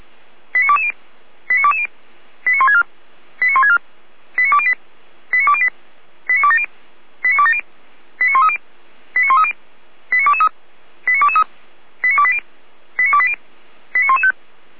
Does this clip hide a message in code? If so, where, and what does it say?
Yes: five-tone selective call (ZVEI) from 0.5 to 14.3 s, heard as quick runs of five tones, tones of about 80 ms. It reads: 89220 89220 89164 89164 89208 89208 89270 89270 89130 89130 89223 89223 89280 89280 89115